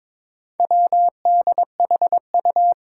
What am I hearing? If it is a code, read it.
Morse code, heard as a keyed tone, about 21 wpm, 700 Hz. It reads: WDHU
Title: Four Challenging Letter-Number Combinations 22wpm